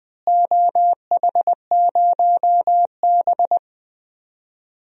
Morse code, heard as a keyed tone, about 19 wpm, 700 Hz. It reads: OH0B
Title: DXCC-100 Call Signs - 4 Characters - Repeated 3x - 20wpm